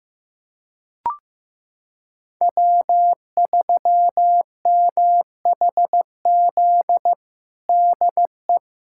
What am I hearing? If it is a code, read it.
Morse code, heard as a keyed tone, about 15 wpm, 700 Hz. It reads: W3MHZ DE
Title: USA Callsign DE Callsign - 5 Characters - 15wpm